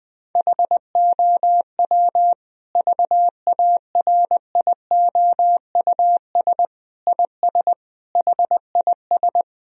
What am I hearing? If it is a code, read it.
Morse code, heard as a keyed tone, about 20 wpm, 700 Hz. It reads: HOW VARIOUS IS HIS
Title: Sentences from Top 1000 Words - Review Part 4 - 20wpm